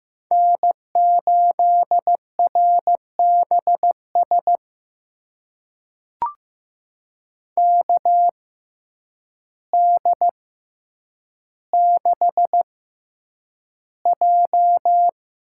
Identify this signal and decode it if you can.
Morse code, heard as a keyed tone, about 15 wpm, spaced as for 8 wpm, 700 Hz. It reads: N8RBS KD6J
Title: USA Call Signs - 6x Character Spacing - 15wpm